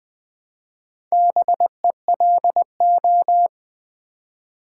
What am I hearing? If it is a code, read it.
Morse code, heard as a keyed tone, about 20 wpm, 700 Hz. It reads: BELO
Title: Build A Word - 1 Letter At A Time - 5-Letter Words - 20wpm